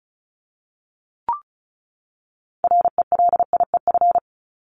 Morse code, heard as a keyed tone, about 34 wpm, 700 Hz. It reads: RELIEF
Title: Top 1501-1750 Words 35wpm